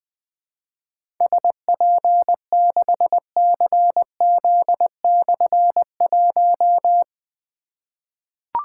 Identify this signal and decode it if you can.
Morse code, heard as a keyed tone, about 20 wpm, 700 Hz. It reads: SP6CZ/1
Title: Difficult Call Signs 20wpm